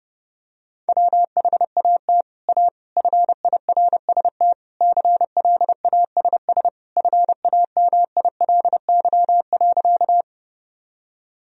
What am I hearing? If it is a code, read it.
Morse code, heard as a keyed tone, about 32 wpm, 700 Hz. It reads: WHAT A FIRST CLASS FAMILY.
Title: Sentences from Top 1000 Words - Review Part 2 - 30wpm